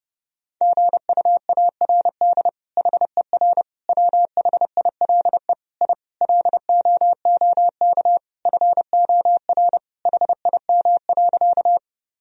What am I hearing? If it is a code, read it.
Morse code, heard as a keyed tone, about 30 wpm, 700 Hz. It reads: GUARD HER WHILE I LOOK FOR HIM.
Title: Sentences from Top 1000 Words - Review Part 4 - 30wpm